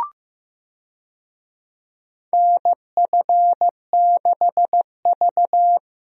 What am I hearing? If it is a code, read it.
Morse code, heard as a keyed tone, about 15 wpm, 700 Hz. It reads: NF6V
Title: USA Call Signs - 4 Characters - Repeated 3x - 15wpm